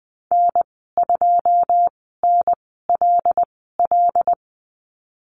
Morse code, heard as a keyed tone, about 20 wpm, spaced as for 15 wpm, 700 Hz. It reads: N2NLL
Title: 082 Lesson 14g - Number 2 - Call Signs 20wpm